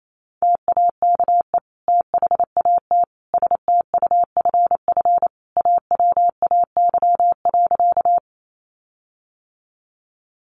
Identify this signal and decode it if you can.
Morse code, heard as a keyed tone, about 28 wpm, 700 Hz. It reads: TAKE THAT STUFF AWAY.
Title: Sentences from Top 1400 Words 28wpm